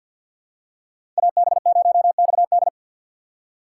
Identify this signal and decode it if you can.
Morse code, heard as a keyed tone, about 53 wpm, 700 Hz. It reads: AB0XD
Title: USA Call Signs - 4x Character Spacing - 50wpm